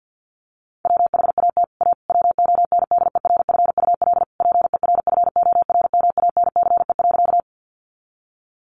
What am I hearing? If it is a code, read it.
Morse code, heard as a keyed tone, about 50 wpm, 700 Hz. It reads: WHAT A WONDERFUL PERFORMANCE.
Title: Sentences from Top 1100 Words 50wpm